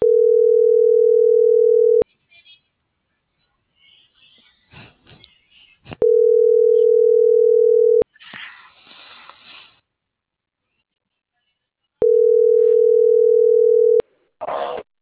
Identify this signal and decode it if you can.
Ringback tone (North America), 440 + 480 Hz.